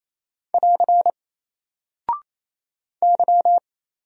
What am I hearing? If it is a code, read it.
Morse code, heard as a keyed tone, about 28 wpm, 700 Hz. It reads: <AR> Y
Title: Single Character (Letter:Number:Punctuation:Prosign) Alt 28wpm